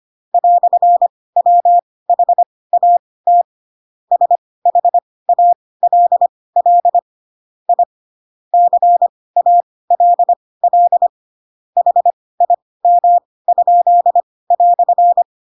Morse code, heard as a keyed tone, about 25 wpm, spaced as for 18 wpm, 700 Hz. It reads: "WHAT SHALL I CALL HIM?"
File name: Pinocchio ch3 12 25 ik1hge